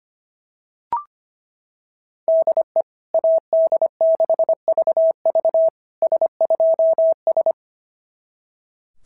Morse code, heard as a keyed tone, about 24 wpm, 650 Hz. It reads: DE AD6VV S2S
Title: SOTA QSO Practice 25wpm